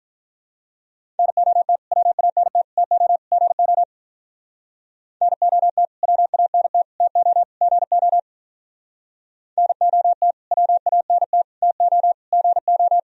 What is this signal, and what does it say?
Morse code, heard as a keyed tone, about 44 wpm, 700 Hz. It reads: NOT WANT TO GO NOT WANT TO GO NOT WANT TO GO